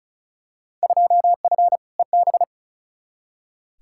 Morse code, heard as a keyed tone, about 36 wpm, 690 Hz. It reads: 2F EB
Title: Field Day Exchanges Encore - 35wpm